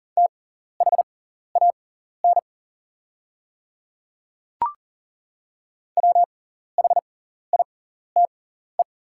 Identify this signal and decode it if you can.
Morse code, heard as a keyed tone, about 40 wpm, spaced as for 14 wpm, 690 Hz. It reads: THAN WHITE
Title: 6x Character Spacing - Top 300 Words - 40wpm